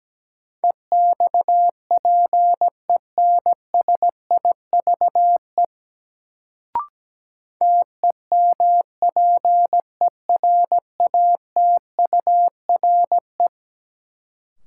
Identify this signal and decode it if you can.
Morse code, heard as a keyed tone, about 17 wpm, 700 Hz. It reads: EXPENSIVE TEMPERATURE
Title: Sentences from Top 1500 Words - New Word Drill Alt - 17wpm